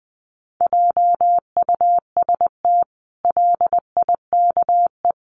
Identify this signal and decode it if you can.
Morse code, heard as a keyed tone, about 20 wpm, 700 Hz. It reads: JUST LIKE
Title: Sentences from Top 1200 Words 20wpm